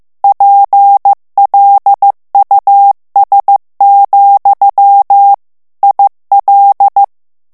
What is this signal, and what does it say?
Morse code, heard as a keyed tone, about 15 wpm, 800 Hz. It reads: PLUS, IL
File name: morse mp3 / FRANCAIS TEXTE CLAIR 1 VITESSE 14 MOTS MINUTES